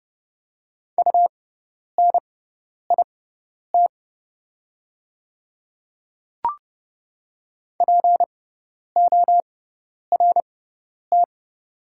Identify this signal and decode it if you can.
Morse code, heard as a keyed tone, about 30 wpm, spaced as for 10 wpm, 700 Hz. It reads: UNIT PORT
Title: 6x Character Spacing - Top 500 Words - 30wpm